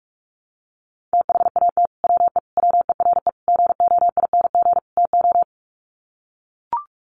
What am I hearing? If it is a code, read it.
Morse code, heard as a keyed tone, about 44 wpm, 700 Hz. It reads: THAT WE WERE GOING TO
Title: Sets of 5 Words Encore2 45wpm